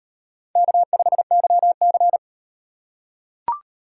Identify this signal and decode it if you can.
Morse code, heard as a keyed tone, about 38 wpm, 700 Hz. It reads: K5YC